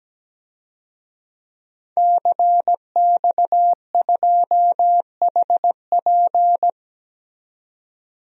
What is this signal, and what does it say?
Morse code, heard as a keyed tone, about 17 wpm, 700 Hz. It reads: CX2HP
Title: DXCC-100 Call Signs - 5 Characters - 17wpm